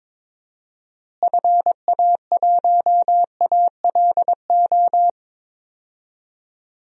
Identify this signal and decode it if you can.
Morse code, heard as a keyed tone, about 21 wpm, 700 Hz. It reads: FA1ALO
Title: DXCC-100 Call Signs - 6 Characters - Repeated 2x - 22wpm